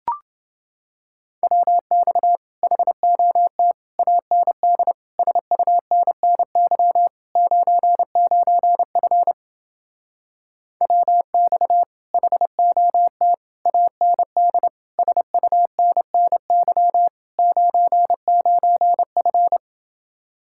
Morse code, heard as a keyed tone, about 29 wpm, 700 Hz. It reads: WX HOT AND SUNNY 99F WX HOT AND SUNNY 99F